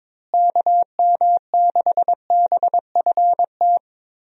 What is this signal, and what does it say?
Morse code, heard as a keyed tone, about 22 wpm, 700 Hz. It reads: KM6BFT
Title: USA Call Signs - 6 Characters - Repeated 3x - 22wpm